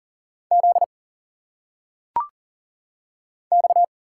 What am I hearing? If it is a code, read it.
Morse code, heard as a keyed tone, about 40 wpm, 700 Hz. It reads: Z X
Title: Single Letters - Phonetic Pronunciation 40wpm